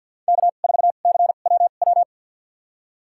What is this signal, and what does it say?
Morse code, heard as a keyed tone, about 53 wpm, spaced as for 37 wpm, 700 Hz. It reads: K4CWW
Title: USA Call Signs - 5 Characters - 2x Character Spacing - 50wpm